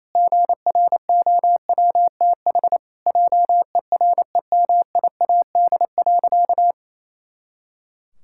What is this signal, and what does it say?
Morse code, heard as a keyed tone, about 28 wpm, 700 Hz. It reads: GROWTH JEREMIAD.
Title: Pangram Challenge v2 - 28wpm